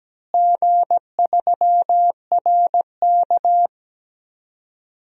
Morse code, heard as a keyed tone, about 17 wpm, 700 Hz. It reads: G3RK
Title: DXCC-100 Call Signs - 4 Characters - Repeated 2x - 17wpm